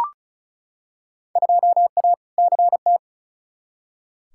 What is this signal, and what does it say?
Morse code, heard as a keyed tone, about 34 wpm, 700 Hz. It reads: 2A CT
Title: Field Day Exchanges Encore - 35wpm